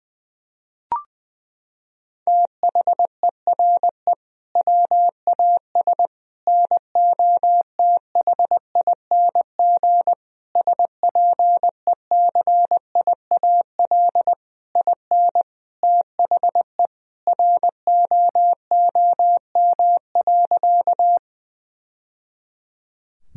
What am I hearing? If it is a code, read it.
Morse code, heard as a keyed tone, about 20 wpm, 700 Hz. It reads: THERE WAS NOTHING SPECIAL IN THE ROOM.
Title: Sentences from Top 1000 Words - Review Part 3 - 20wpm